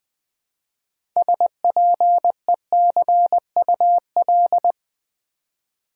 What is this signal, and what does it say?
Morse code, heard as a keyed tone, about 20 wpm, 700 Hz. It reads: SPECUL
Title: Build A Word - 3 Letters At A Time - 9-Letter Words - 20wpm